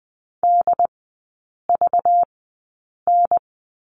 Morse code, heard as a keyed tone, about 20 wpm, spaced as for 8 wpm, 700 Hz. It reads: DVN
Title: 191 Lesson 31e - Letter v - Three Characters 20wpm